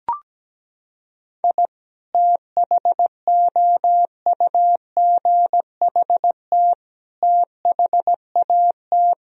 Morse code, heard as a keyed tone, about 17 wpm, 700 Hz. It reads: I THOUGHT THAT
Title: Sentences from Top 1000 Words - Review Part 4 - 17wpm